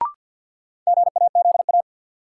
Morse code, heard as a keyed tone, about 52 wpm, 700 Hz. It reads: GAZA
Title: Sets of 5 Words - New Word Drill - 50wpm